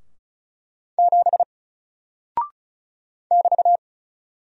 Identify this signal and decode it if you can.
Morse code, heard as a keyed tone, about 35 wpm, 700 Hz. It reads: 7 <BT>
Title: Single Character (Letter:Number:Punctuation:Prosign) 35wpm